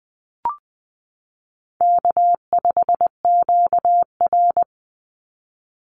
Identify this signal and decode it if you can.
Morse code, heard as a keyed tone, about 20 wpm, 700 Hz. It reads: K5QR
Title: NA SOTA Chaser Callsigns 20wpm